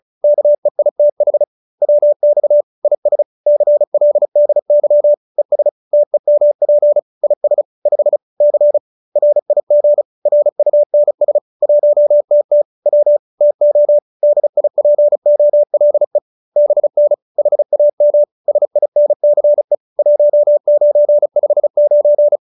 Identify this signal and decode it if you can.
Morse code, heard as a keyed tone, about 34 wpm, 590 Hz. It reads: KEITH WX IS CLDY ES TEMP IS 5 C RIG RUNS 1TT W TO DIPOLE BN HAM SINCE 1959